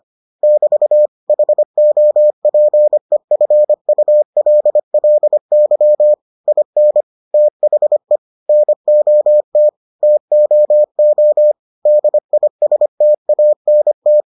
Morse code, heard as a keyed tone, about 25 wpm, 600 Hz. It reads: <BT> HOPEFULLY IN THE NOT TOO DISTANT